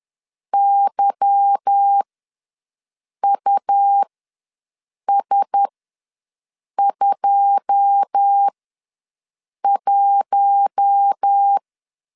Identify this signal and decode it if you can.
Morse code, heard as a keyed tone, about 10 wpm, spaced as for 6 wpm, 800 Hz. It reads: YUS21
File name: LEZIONI / 9 2 ws 10wpm 4130